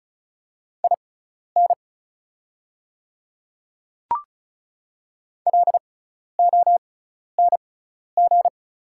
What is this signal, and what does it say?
Morse code, heard as a keyed tone, about 36 wpm, spaced as for 12 wpm, 690 Hz. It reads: IN LONG
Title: 6x Character Spacing - Top 100 Words - 35wpm